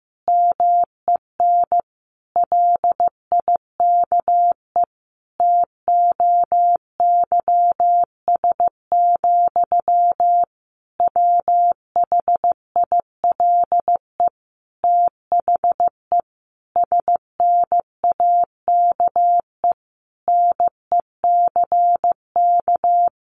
Morse code, heard as a keyed tone, about 15 wpm, 700 Hz. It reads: MEN LIKE TOYS, WHILE THE SNAKE NECK
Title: A Sound Of Thunder 15wpm